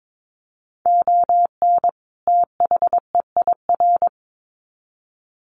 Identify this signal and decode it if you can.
Morse code, heard as a keyed tone, about 22 wpm, 700 Hz. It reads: ON THEIR
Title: Sets of 2 Words 22wpm